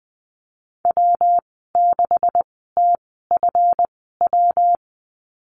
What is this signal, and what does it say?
Morse code, heard as a keyed tone, about 19 wpm, spaced as for 14 wpm, 700 Hz. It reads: W6TFW
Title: 219 Lesson 35g - Number 0 - Call Signs 20wpm